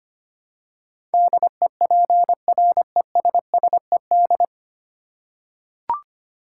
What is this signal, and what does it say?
Morse code, heard as a keyed tone, about 25 wpm, 700 Hz. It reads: DEPRESSED
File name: 9 Letter Words - First 3 Letters Given - 25wpm